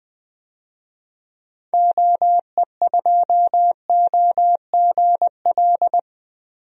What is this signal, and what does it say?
Morse code, heard as a keyed tone, about 21 wpm, 700 Hz. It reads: OE2OGL
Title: DXCC-100 Call Signs - 6 Characters - Repeated 2x - 20wpm